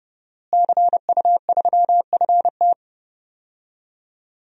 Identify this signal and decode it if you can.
Morse code, heard as a keyed tone, about 32 wpm, 700 Hz. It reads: CU3FT
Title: DXCC-100 Call Signs - 5 Characters - Repeated 3x - 30wpm